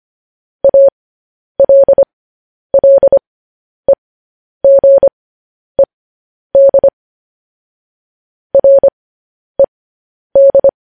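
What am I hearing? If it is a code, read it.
Morse code, heard as a keyed tone, about 25 wpm, spaced as for 10 wpm, 550 Hz. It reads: ALLEGED RED
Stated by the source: Morse code transmission of news headlines.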